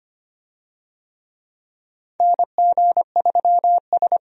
Morse code, heard as a keyed tone, about 25 wpm, 700 Hz. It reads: NG3S
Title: USA Call Signs - 4 Characters - 25wpm